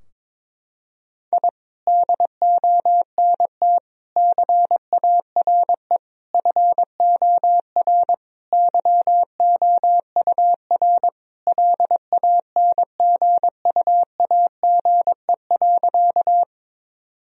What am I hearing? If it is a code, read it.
Morse code, heard as a keyed tone, about 22 wpm, 700 Hz. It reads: I DONT CARE FOR YOUR LANGUAGE.
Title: Sentences from Top 1000 Words - Review Part 3 - 22wpm